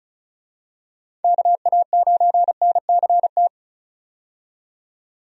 Morse code, heard as a keyed tone, about 36 wpm, 700 Hz. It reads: KA9NCT
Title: USA Call Signs - 6 Characters - 35wpm